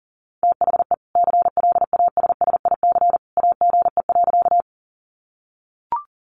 Morse code, heard as a keyed tone, about 40 wpm, 700 Hz. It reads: THE CLASSIC AGE.
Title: Sentences from Top 1000 Words 40wpm